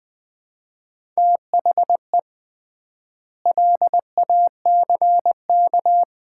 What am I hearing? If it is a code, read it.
Morse code, heard as a keyed tone, about 21 wpm, 700 Hz. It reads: THE LACK
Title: 3x Word-Spacing - Sets of 3 Words - 20wpm